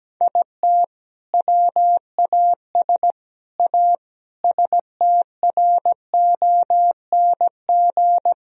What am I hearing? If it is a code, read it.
Morse code, heard as a keyed tone, about 17 wpm, 700 Hz. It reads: IT WAS A STRONG
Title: Sentences from Top 1000 Words - Review Part 2 - 17wpm